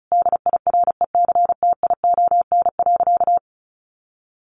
Morse code, heard as a keyed tone, about 34 wpm, 700 Hz. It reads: DIRECTION.
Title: Sentences from Top 1300 Words 35wpm